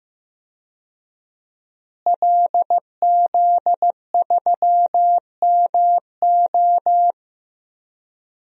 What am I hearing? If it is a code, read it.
Morse code, heard as a keyed tone, about 15 wpm, 700 Hz. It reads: LZ3MO
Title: DXCC-100 Call Signs - 5 Characters - Repeated 2x - 15wpm